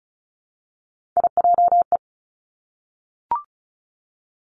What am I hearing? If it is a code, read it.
Morse code, heard as a keyed tone, about 36 wpm, 700 Hz. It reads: IJE
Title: Three Challenging Letter-Number Combinations 35wpm